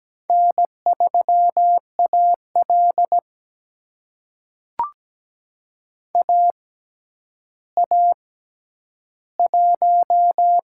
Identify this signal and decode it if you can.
Morse code, heard as a keyed tone, about 17 wpm, spaced as for 10 wpm, 700 Hz. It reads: N3AL AA1
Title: USA Call Signs - 6x Character Spacing - 17wpm